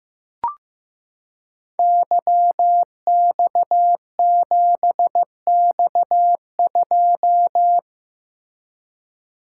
Morse code, heard as a keyed tone, about 15 wpm, 700 Hz. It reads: YX7X2
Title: Code Groups - 5 Letter:Numbers - 15wpm